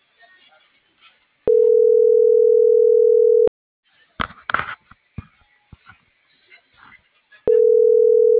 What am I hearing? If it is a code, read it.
Ringback tone (North America), 440 + 480 Hz.